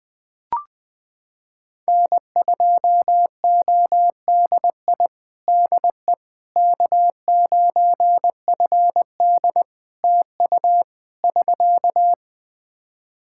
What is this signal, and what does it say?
Morse code, heard as a keyed tone, about 20 wpm, 700 Hz. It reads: N2ODI DE K9FD TU <SK>
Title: Practice QSO - Exchange 4 - Ending - 20wpm